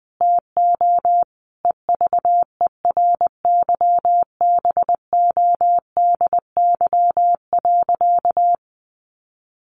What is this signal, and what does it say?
Morse code, heard as a keyed tone, about 19 wpm, 700 Hz. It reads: TO EVERYBODY.
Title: Sentences from Top 1400 Words 20wpm